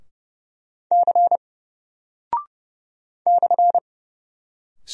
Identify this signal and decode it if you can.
Morse code, heard as a keyed tone, about 30 wpm, 700 Hz. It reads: C /
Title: Single Character (Letter:Number:Punctuation:Prosign) 30wpm